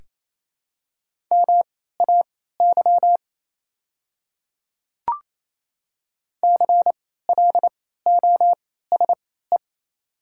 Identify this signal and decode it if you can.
Morse code, heard as a keyed tone, about 28 wpm, spaced as for 16 wpm, 700 Hz. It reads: MAY CLOSE
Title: 3x Character Spacing - Top 300 Words - 28wpm